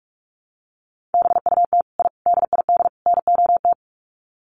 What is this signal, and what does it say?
Morse code, heard as a keyed tone, about 44 wpm, 700 Hz. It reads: BUT I DID NOT
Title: Sets of 4 Words 45wpm